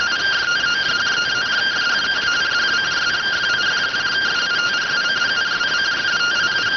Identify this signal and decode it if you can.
Two-tone teleprinter signal, about 170 Hz shift.